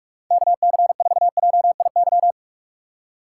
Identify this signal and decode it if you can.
Morse code, heard as a keyed tone, about 46 wpm, 700 Hz. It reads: KC4JIY
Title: USA Call Signs - 6 Characters - 3x Character Spacing - 45wpm